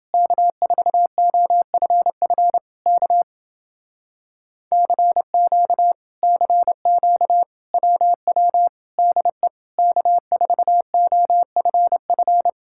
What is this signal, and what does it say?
Morse code, heard as a keyed tone, about 29 wpm, 700 Hz. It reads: K4OFF K CQ CQ WW DE K4OFF